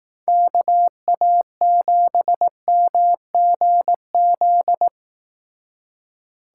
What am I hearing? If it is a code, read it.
Morse code, heard as a keyed tone, about 18 wpm, 700 Hz. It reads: KA7MGZ
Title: Speed Racing - USA 2x3 Call Signs 15wpm